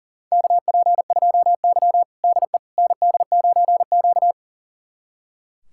Morse code, heard as a keyed tone, about 38 wpm, 700 Hz. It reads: KP2Y DE ND9Q
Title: USA Callsign DE Callsign - 4 Characters - 40wpm